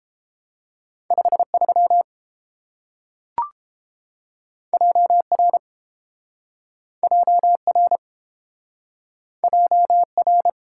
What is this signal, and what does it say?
Morse code, heard as a keyed tone, about 30 wpm, 700 Hz. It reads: 53 JR JR JR